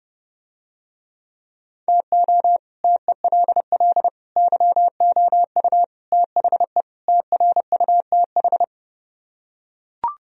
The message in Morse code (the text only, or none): TO TELL YOU THE TRUTH